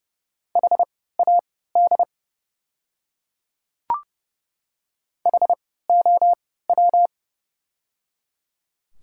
Morse code, heard as a keyed tone, about 30 wpm, spaced as for 17 wpm, 700 Hz. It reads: HAD HOW
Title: 3x Character Spacing - Top 100 Words - 30wpm